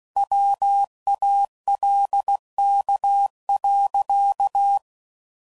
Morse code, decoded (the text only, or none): WALK.